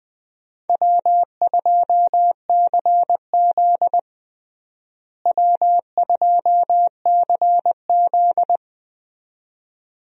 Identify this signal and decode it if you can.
Morse code, heard as a keyed tone, about 20 wpm, 700 Hz. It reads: W2CZ W2CZ
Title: USA Call Signs - 4 Characters - Repeated 2x - 20wpm